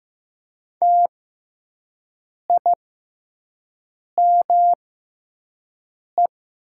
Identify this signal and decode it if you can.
Morse code, heard as a keyed tone, about 15 wpm, spaced as for 5 wpm, 700 Hz. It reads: TIME
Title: 6x Character Spacing - Top 100 Words - 15wpm